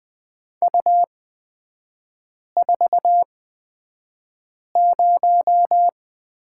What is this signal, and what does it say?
Morse code, heard as a keyed tone, about 20 wpm, spaced as for 5 wpm, 690 Hz. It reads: U40
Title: 236 Lesson 38e - Letter q - Three Characters 20wpm